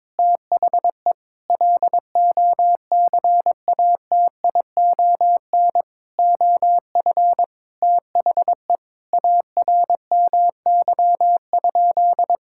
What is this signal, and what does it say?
Morse code, heard as a keyed tone, about 22 wpm, 700 Hz. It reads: THE LOCATION OF THE ARMY?
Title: Sentences from Top 1000 Words - Review Part 6 - 22wpm